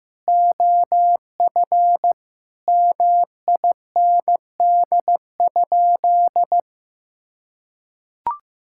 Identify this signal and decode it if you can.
Morse code, heard as a keyed tone, about 15 wpm, 700 Hz. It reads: OF MIND?
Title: Sentences from Top 1000 Words - Review Part 5 - 15wpm